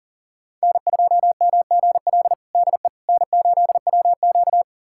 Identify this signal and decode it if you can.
Morse code, heard as a keyed tone, about 40 wpm, 700 Hz. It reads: N2MGL DE N8WQ
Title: USA Callsign DE Callsign 40wpm